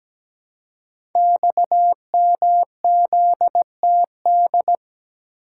Morse code, heard as a keyed tone, about 17 wpm, 690 Hz. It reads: XMZTD
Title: Code Groups - 5 Letter:Numbers - 17wpm